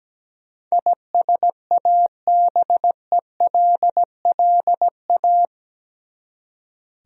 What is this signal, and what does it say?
Morse code, heard as a keyed tone, about 17 wpm, 700 Hz. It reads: ISABELLA
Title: USA Common Names - 17wpm